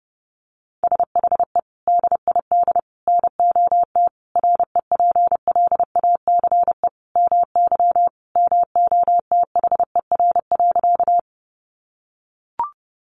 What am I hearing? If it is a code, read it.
Morse code, heard as a keyed tone, about 30 wpm, 700 Hz. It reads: SHE DID NOT REPLACE MY MOTHER.
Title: Sentences from Top 1400 Words 30wpm